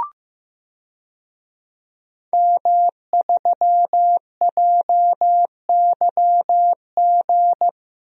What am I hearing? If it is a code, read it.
Morse code, heard as a keyed tone, about 14 wpm, 700 Hz. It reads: M3JYG
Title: DXCC-100 Call Signs - 5 Characters - Repeated 3x - 15wpm